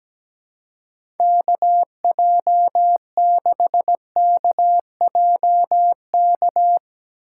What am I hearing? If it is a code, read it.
Morse code, heard as a keyed tone, about 17 wpm, 700 Hz. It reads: K J 6 K J K
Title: USA Call Signs - 6 Characters - 6x Character Spacing - 17wpm